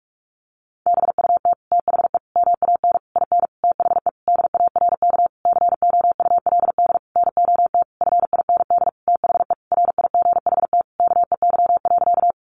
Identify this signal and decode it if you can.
Morse code, heard as a keyed tone, about 44 wpm, 700 Hz. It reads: BUT THE MAN IN THE DARK COULD NOT FIND THE RIGHT KEY.
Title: Sentences from Top 600 Words 45wpm